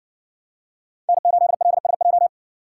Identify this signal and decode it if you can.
Morse code, heard as a keyed tone, about 60 wpm, 700 Hz. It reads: N8LSJ